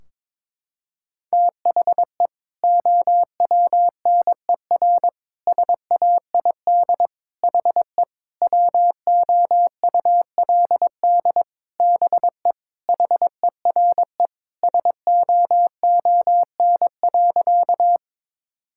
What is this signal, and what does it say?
Morse code, heard as a keyed tone, about 22 wpm, 700 Hz. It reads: THE OWNER SAID HE WOULD BE HERE SOON.
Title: Sentences from Top 1600 Words 22wpm